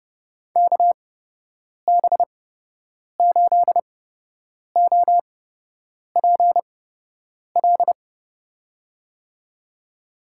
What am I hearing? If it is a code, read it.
Morse code, heard as a keyed tone, about 32 wpm, spaced as for 8 wpm, 700 Hz. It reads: KB8OPL
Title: USA Call Signs - 6 Characters - 8x Character Spacing - 30wpm